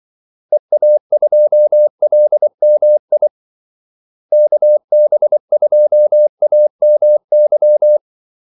Morse code, heard as a keyed tone, about 24 wpm, 600 Hz. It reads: EA2LMI KB2AMY
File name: Callsigns-24wpm